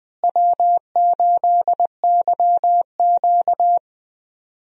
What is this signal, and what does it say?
Morse code, heard as a keyed tone, about 20 wpm, 700 Hz. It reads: W8YQ
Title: USA Call Signs - 4 Characters - Repeated 3x - 20wpm